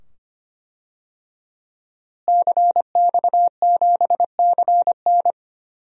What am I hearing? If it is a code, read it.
Morse code, heard as a keyed tone, about 25 wpm, 700 Hz. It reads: CX7CN
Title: DXCC-100 Call Signs - 5 Characters - Repeated 2x - 25wpm